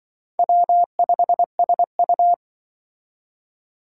Morse code, heard as a keyed tone, about 25 wpm, 700 Hz. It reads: W5SU
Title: Speed Racing - USA 1x2 Call Signs 20wpm